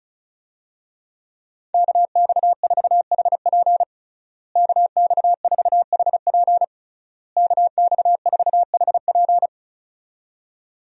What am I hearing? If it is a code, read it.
Morse code, heard as a keyed tone, about 36 wpm, 700 Hz. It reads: KX4HP KX4HP KX4HP
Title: USA Call Signs - 5 Characters - Repeated 3x - 35wpm